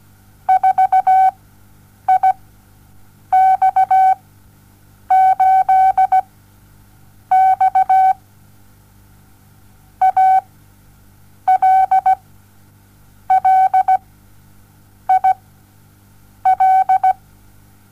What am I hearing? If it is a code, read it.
Morse code, heard as a keyed tone, about 17 wpm, spaced as for 7 wpm, 760 Hz. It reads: 4IX8X ALLIL
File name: Onlinelehrgang / morsen / DJ4UF Morsen07